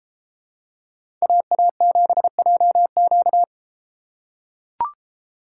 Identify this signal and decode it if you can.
Morse code, heard as a keyed tone, about 33 wpm, 700 Hz. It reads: AA7JQ